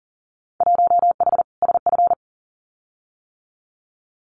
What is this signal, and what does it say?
Morse code, heard as a keyed tone, about 40 wpm, 700 Hz. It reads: 1H SF
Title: Winter Field Day Exchanges 40wpm